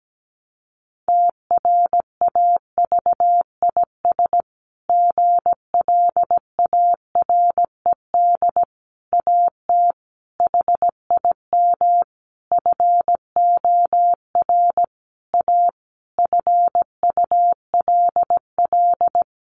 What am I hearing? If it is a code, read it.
Morse code, heard as a keyed tone, about 17 wpm, 700 Hz. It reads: TRAVIS GLARED AT HIM FOR A FULL
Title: A Sound Of Thunder 17wpm